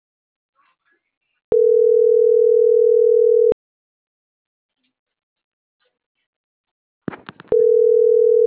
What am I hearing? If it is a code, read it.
Ringback tone (North America), 440 + 480 Hz.